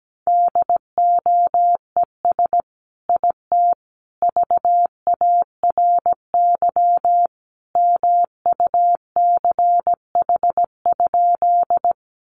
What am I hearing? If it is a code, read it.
Morse code, heard as a keyed tone, about 17 wpm, 700 Hz. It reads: DOES IT VARY MUCH?
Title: Sentences from Top 1200 Words 17wpm